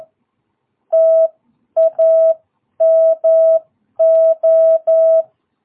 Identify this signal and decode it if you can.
Morse code, heard as a keyed tone, about 11 wpm, spaced as for 9 wpm, 660 Hz. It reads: TAMO
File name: jh8cbh / 2022cw